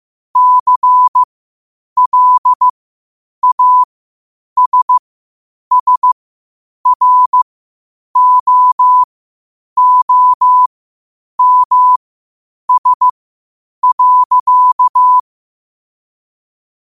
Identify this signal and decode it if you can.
Morse code, heard as a keyed tone, about 15 wpm, spaced as for 8 wpm, 1000 Hz. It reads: CLASSROOMS.
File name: qotdfiles / 8wpm